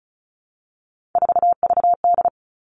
Morse code, heard as a keyed tone, about 34 wpm, 700 Hz. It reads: 4VD
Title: All Cut Numbers - 3-Digit 35wpm